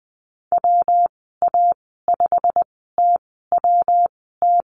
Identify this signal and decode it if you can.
Morse code, heard as a keyed tone, about 20 wpm, spaced as for 15 wpm, 700 Hz. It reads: WA5TWT
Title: 122 Lesson 20g - Number 3 - Call Signs 20wpm